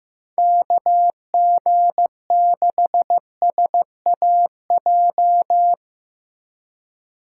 Morse code, heard as a keyed tone, about 17 wpm, 700 Hz. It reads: KG6SAJ
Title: Speed Racing - USA 2x3 Call Signs 15wpm